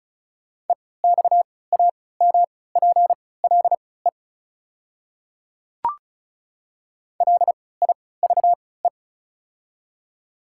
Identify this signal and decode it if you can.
Morse code, heard as a keyed tone, about 36 wpm, spaced as for 20 wpm, 690 Hz. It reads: EXAMPLE LIVE
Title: 3x Character Spacing - Top 300 Words - 35wpm